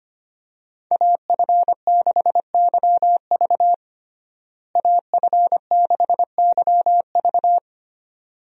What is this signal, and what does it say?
Morse code, heard as a keyed tone, about 25 wpm, 690 Hz. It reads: AF6YV AF6YV
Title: USA Call Signs - 5 Characters - Repeated 2x - 25wpm